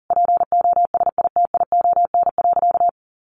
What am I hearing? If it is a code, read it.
Morse code, heard as a keyed tone, about 40 wpm, 700 Hz. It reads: POSITION.
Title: Sentences from Top 1400 Words 40wpm